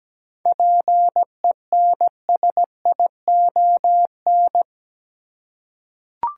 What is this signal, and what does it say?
Morse code, heard as a keyed tone, about 17 wpm, 700 Hz. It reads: PENSION
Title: Seven Letter Words - 17wpm